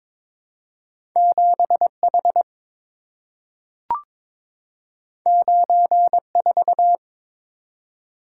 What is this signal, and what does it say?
Morse code, heard as a keyed tone, about 22 wpm, 700 Hz. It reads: 7H 94
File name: practice / 22wpm / Two Letter-Number Combinations - 22wpm